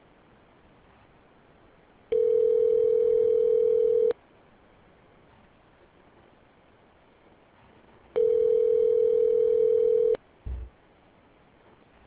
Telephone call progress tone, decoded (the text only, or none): ringback tone